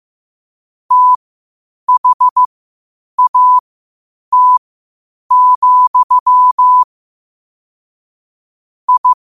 Morse code, heard as a keyed tone, about 15 wpm, spaced as for 8 wpm, 1000 Hz. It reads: THAT, I
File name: qotdfiles / 8wpm